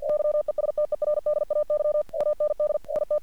Morse code, heard as a keyed tone, about 48 wpm, 600 Hz. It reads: YESTERDAY AND IT
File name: archived / CRSnet